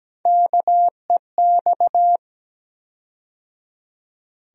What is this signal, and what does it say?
Morse code, heard as a keyed tone, about 17 wpm, 700 Hz. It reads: KEX
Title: USA Call Sign Postfixes - 3 Letters 17wpm